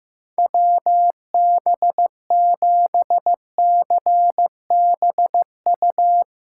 Morse code, heard as a keyed tone, about 15 wpm, 700 Hz. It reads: WB7CBU
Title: USA Callsign DE Callsign - 6 Characters - 15wpm